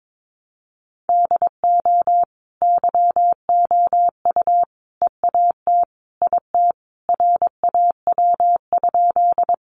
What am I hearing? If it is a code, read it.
Morse code, heard as a keyed tone, about 22 wpm, 700 Hz. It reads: DO YOU EAT IT RAW?
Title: Sentences from Top 1500 Words 22wpm